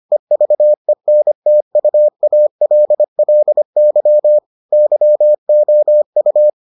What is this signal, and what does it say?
Morse code, heard as a keyed tone, about 25 wpm, 600 Hz. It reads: EVENTUALLY YOU